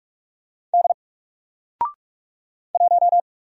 Morse code, heard as a keyed tone, about 44 wpm, 700 Hz. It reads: D 1
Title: Single Letter-Number Alt 45wpm